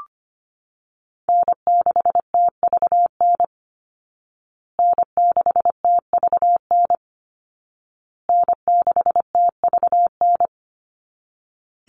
Morse code, heard as a keyed tone, about 25 wpm, 700 Hz. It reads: N6TVN N6TVN N6TVN
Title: Call Signs Repeated 2x - 25wpm